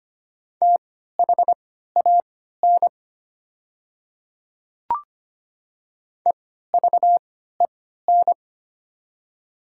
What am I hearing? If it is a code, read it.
Morse code, heard as a keyed tone, about 25 wpm, spaced as for 14 wpm, 700 Hz. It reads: THAN EVEN
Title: 3x Character Spacing - Top 200 Words - 25wpm